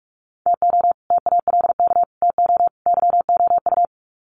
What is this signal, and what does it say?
Morse code, heard as a keyed tone, about 44 wpm, 700 Hz. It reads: TO TALK TO YOU
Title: Sets of 4 Words 45wpm